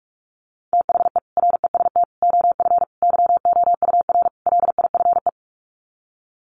Morse code, heard as a keyed tone, about 44 wpm, 700 Hz. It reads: THE REST OF YOUR LIFE
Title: Sets of 5 Words 45wpm